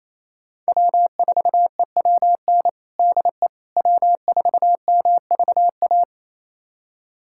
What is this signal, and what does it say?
Morse code, heard as a keyed tone, about 28 wpm, 700 Hz. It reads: W4EWN DE W4MVA
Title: USA Callsign DE Callsign - 5 Characters - 28wpm